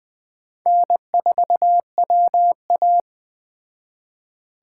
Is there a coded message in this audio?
Yes: Morse code from 0.7 to 3.0 s, heard as a keyed tone, about 19 wpm, 700 Hz. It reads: N4WA